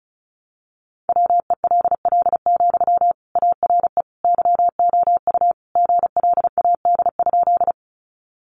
Morse code, heard as a keyed tone, about 34 wpm, 700 Hz. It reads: WELL, ARE YOU GLAD?
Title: Sentences from Top 800 Words 35wpm